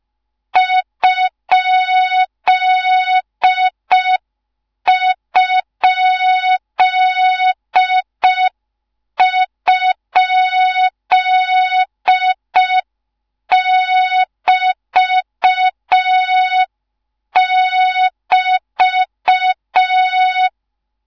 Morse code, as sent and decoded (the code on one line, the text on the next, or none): ..--.. ..--.. ..--.. -...- -...-
???<BT><BT>